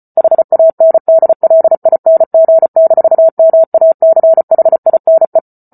Morse code, heard as a keyed tone, about 34 wpm, 650 Hz. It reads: HANDLING-MACHINE